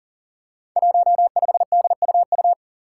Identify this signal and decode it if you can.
Morse code, heard as a keyed tone, about 40 wpm, 700 Hz. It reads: 15DUU
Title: Five Challenging Letter-Number Combinations 40wpm